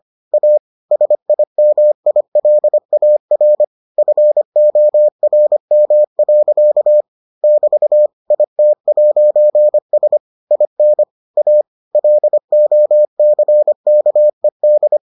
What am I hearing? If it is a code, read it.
Morse code, heard as a keyed tone, about 25 wpm, 600 Hz. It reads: A SIMILAR FORM. <BT> IT'S IN A LOCKED